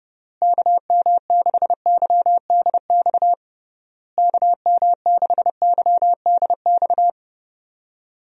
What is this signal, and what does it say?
Morse code, heard as a keyed tone, about 32 wpm, 700 Hz. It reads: KM6YDX KM6YDX
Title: USA Call Signs - 6 Characters - Repeated 3x - 30wpm